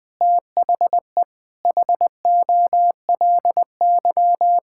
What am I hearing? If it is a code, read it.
Morse code, heard as a keyed tone, about 20 wpm, 700 Hz. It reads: THE HOLY
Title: Sentences with 9-Word Phrases - 20wpm